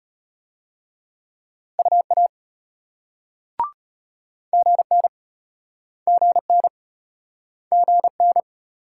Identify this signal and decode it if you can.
Morse code, heard as a keyed tone, about 36 wpm, 700 Hz. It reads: UA GN GN GN